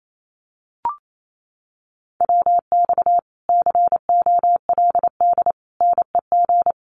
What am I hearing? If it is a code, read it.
Morse code, heard as a keyed tone, about 28 wpm, 700 Hz. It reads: WX COLD NEG
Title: Common QSO Phrases Encore 28wpm